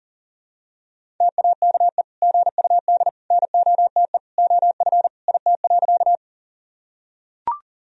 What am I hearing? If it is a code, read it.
Morse code, heard as a keyed tone, about 40 wpm, 700 Hz. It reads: TAKE GUD NOTE OF IT.
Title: Sentences from Top 500 Words and Abbreviations 40wpm